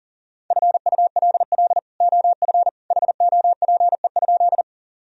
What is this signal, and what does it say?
Morse code, heard as a keyed tone, about 37 wpm, 700 Hz. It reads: FULL OF HOPE?
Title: Sentences from Top 1000 Words - Review Part 3 - 40wpm